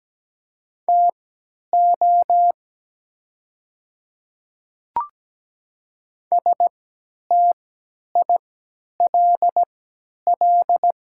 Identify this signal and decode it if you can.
Morse code, heard as a keyed tone, about 17 wpm, spaced as for 10 wpm, 700 Hz. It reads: TO STILL
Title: 3x Character Spacing - Top 500 Words - 17wpm